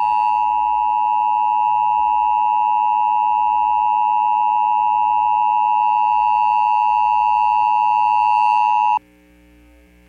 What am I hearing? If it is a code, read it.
Emergency Alert System (SAME) tones (the 853 and 960 Hz attention tone). It is heard from 0.7 to 9.0 s.